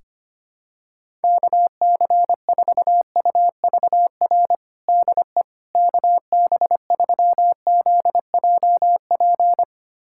Morse code, heard as a keyed tone, about 25 wpm, 700 Hz. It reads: KC4UVR DE KB3ZJP
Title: USA Callsign DE Callsign 25wpm